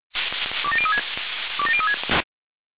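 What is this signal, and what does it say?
Five-tone selective call (ZVEI), heard as quick runs of five tones, tones of about 70 ms. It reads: 27036 27036
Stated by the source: ZVEI five-tone selective call 27002 received on SDR.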